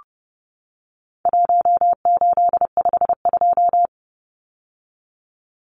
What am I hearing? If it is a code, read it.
Morse code, heard as a keyed tone, about 30 wpm, 700 Hz. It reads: 1852
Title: Numbers - 4-Digit 30wpm